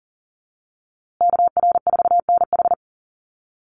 Morse code, heard as a keyed tone, about 40 wpm, 700 Hz. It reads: KR4NH
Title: USA Call Signs Repeated 3x 40wpm